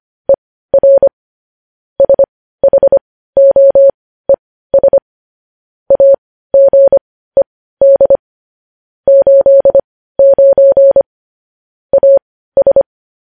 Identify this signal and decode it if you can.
Morse code, heard as a keyed tone, about 25 wpm, spaced as for 15 wpm, 550 Hz. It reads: ER SHOES AGED 89 AS
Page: News Headlines in Morse Code at 15 WPM